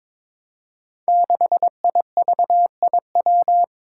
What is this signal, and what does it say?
Morse code, heard as a keyed tone, about 22 wpm, 700 Hz. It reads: 6IVIW
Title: Five Challenging Letter-Number Combinations 22wpm